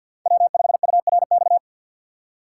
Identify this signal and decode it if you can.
Morse code, heard as a keyed tone, about 52 wpm, 700 Hz. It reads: W5URX
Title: USA Call Signs - 2x Character Spacing - 50wpm